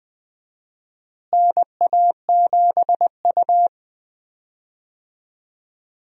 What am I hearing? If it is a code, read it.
Morse code, heard as a keyed tone, about 21 wpm, 700 Hz. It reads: NA7U
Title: USA Call Signs - 4 Characters - Repeated 2x - 20wpm